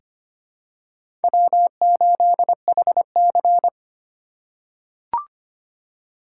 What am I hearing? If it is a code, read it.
Morse code, heard as a keyed tone, about 25 wpm, 700 Hz. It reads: W8HC
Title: USA Call Signs - 2x Character Spacing - 25wpm